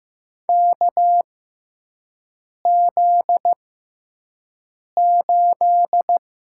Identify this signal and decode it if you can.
Morse code, heard as a keyed tone, about 15 wpm, spaced as for 5 wpm, 700 Hz. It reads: KZ8
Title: USA Call Signs - 4 Characters - 6x Character Spacing - 15wpm